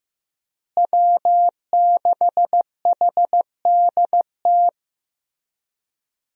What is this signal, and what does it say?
Morse code, heard as a keyed tone, about 15 wpm, 700 Hz. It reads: W6HDT
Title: USA Call Signs - 5 Characters - Repeated 3x - 15wpm